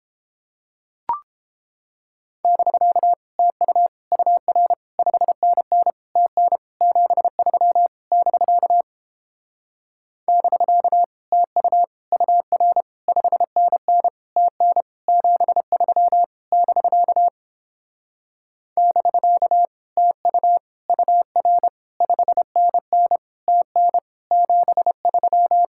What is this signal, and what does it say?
Morse code, heard as a keyed tone, about 29 wpm, 700 Hz. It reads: <BK> TU UR 5NN TN 73 <BK> <BK> TU UR 5NN TN 73 <BK> <BK> TU UR 5NN TN 73